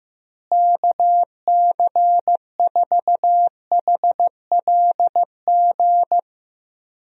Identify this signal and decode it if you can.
Morse code, heard as a keyed tone, about 16 wpm, 700 Hz. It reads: KC4HLG
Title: USA Call Signs - 6 Characters - Repeated 3x - 15wpm